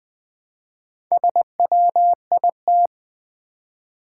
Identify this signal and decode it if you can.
Morse code, heard as a keyed tone, about 20 wpm, 690 Hz. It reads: SWIT
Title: Build A Word - 2 Letters At A Time - 6-Letter Words - 20wpm